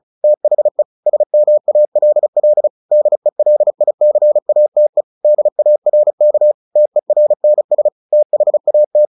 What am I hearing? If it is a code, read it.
Morse code, heard as a keyed tone, about 34 wpm, 600 Hz. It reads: THE SMALL DELICATE DARK TERNS THAT
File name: the old man and the sea / the old man and the sea 1 35wpm